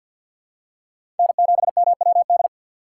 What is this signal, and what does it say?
Morse code, heard as a keyed tone, about 47 wpm, 700 Hz. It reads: N7MWD
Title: USA Call Signs - 5 Characters - 3x Character Spacing - 50wpm